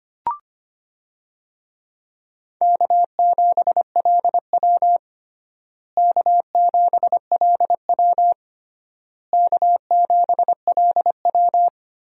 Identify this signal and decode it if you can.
Morse code, heard as a keyed tone, about 25 wpm, 700 Hz. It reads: K7LW K7LW K7LW
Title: USA Call Signs - 4 Characters - Repeated 3x - 25wpm